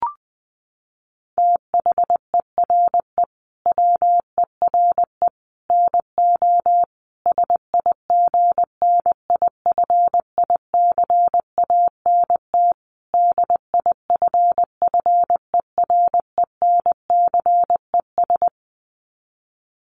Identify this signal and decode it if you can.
Morse code, heard as a keyed tone, about 20 wpm, 700 Hz. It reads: THERE WERE NO SIGNIFICANT DIFFERENCES
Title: Sets of 5 Words Encore2 20wpm